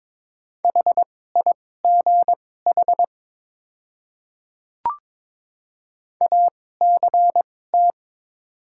Morse code, heard as a keyed tone, about 21 wpm, spaced as for 16 wpm, 700 Hz. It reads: HIGH ACT
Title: 2x Character Spacing - Top 200 Words - 22wpm